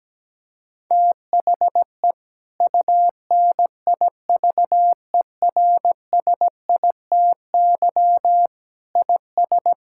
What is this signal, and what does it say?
Morse code, heard as a keyed tone, about 17 wpm, 700 Hz. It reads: THE UNIVERSITY IS
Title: Sentences from Top 1000 Words - Review Part 5 - 17wpm